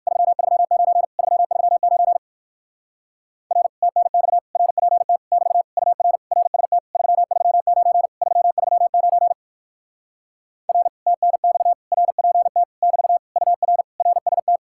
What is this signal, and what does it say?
Morse code, heard as a keyed tone, about 58 wpm, 700 Hz. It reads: 339 339 R TNX RPT <BT> UR RST 339 339 R TNX RPT <BT> UR RST